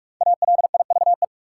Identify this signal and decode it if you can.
Morse code, heard as a keyed tone, about 46 wpm, 700 Hz. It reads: ALIVE
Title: Five Letter Words - 45wpm